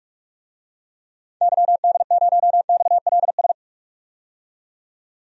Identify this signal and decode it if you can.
Morse code, heard as a keyed tone, about 42 wpm, 700 Hz. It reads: YD0XLS
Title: DXCC-100 Call Signs - 6 Characters - 45wpm